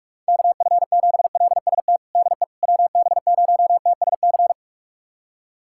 Morse code, heard as a keyed tone, about 42 wpm, 700 Hz. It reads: KF7LST DE WB0TSC
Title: USA Callsign DE Callsign 45wpm